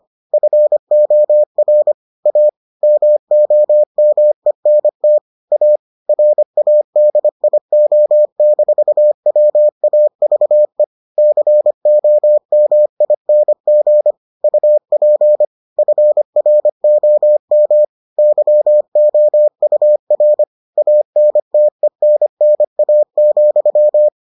Morse code, heard as a keyed tone, about 25 wpm, 600 Hz. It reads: FOR A MOMENT A RADIO-WAVE COMING UP FROM YOUR ANTENNA,